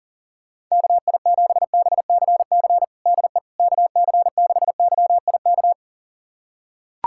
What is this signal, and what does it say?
Morse code, heard as a keyed tone, about 40 wpm, 700 Hz. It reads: KI7BCC DE KC6YIK
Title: USA Callsign DE Callsign 40wpm